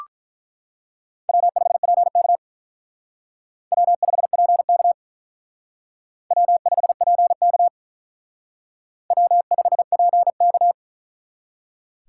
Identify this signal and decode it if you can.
Morse code, heard as a keyed tone, about 43 wpm, 700 Hz. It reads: W5PK W5PK W5PK W5PK
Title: Speed Racing - USA 1x2 Call Signs 35wpm